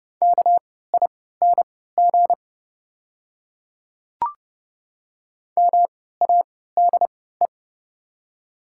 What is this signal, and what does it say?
Morse code, heard as a keyed tone, about 29 wpm, spaced as for 17 wpm, 700 Hz. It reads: KING MADE
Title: 3x Character Spacing - Top 500 Words - 30wpm